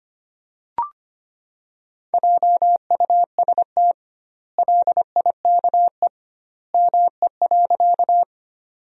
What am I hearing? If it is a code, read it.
Morse code, heard as a keyed tone, about 25 wpm, 700 Hz. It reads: JUST LIKE ME.
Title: 2x Word-Spacing - Sentences from Top 200 Words - 25wpm